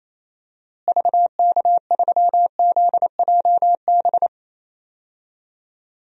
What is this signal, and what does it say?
Morse code, heard as a keyed tone, about 27 wpm, 700 Hz. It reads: VK3ZJB
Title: DXCC-100 Call Signs - 6 Characters - Repeated 2x - 28wpm